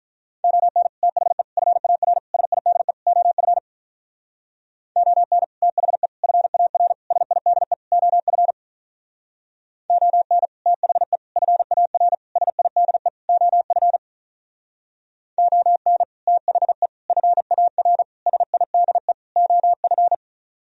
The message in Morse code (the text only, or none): ON THE FAR SIDE OF ON THE FAR SIDE OF ON THE FAR SIDE OF ON THE FAR SIDE OF